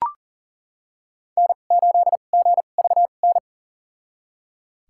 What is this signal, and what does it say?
Morse code, heard as a keyed tone, about 37 wpm, spaced as for 28 wpm, 700 Hz. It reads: N8GVN
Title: USA Call Signs - 5 Characters - 2x Character Spacing - 40wpm